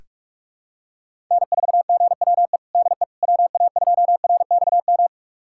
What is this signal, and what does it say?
Morse code, heard as a keyed tone, about 42 wpm, 700 Hz. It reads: N4GWE DE WA2RXM
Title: USA Callsign DE Callsign 45wpm